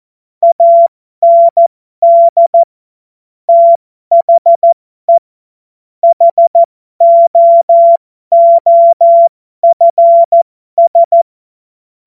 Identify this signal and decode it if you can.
Morse code, heard as a keyed tone, about 14 wpm, spaced as for 12 wpm, 680 Hz. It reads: AND THE HOOFS